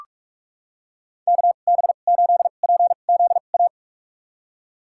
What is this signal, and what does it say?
Morse code, heard as a keyed tone, about 46 wpm, spaced as for 32 wpm, 700 Hz. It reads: KB8PZA
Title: USA Call Signs - 6 Characters - 2x Character Spacing - 45wpm